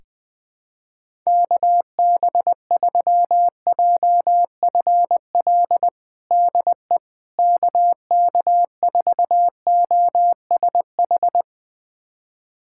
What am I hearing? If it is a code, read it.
Morse code, heard as a keyed tone, about 20 wpm, 700 Hz. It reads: KB3JFL DE KK4OSH
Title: USA Callsign DE Callsign 20wpm